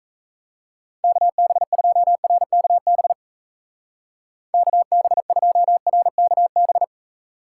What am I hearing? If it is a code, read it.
Morse code, heard as a keyed tone, about 34 wpm, 700 Hz. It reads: KB2RKB KB2RKB
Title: Speed Racing - USA 2x3 Call Signs 28wpm